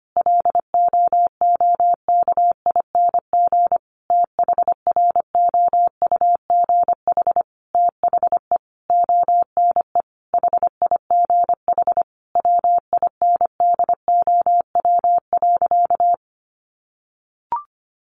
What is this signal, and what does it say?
Morse code, heard as a keyed tone, about 25 wpm, 700 Hz. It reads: LOOKING THROUGH THE ONE HIGH WINDOW.
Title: A Sound Of Thunder 25wpm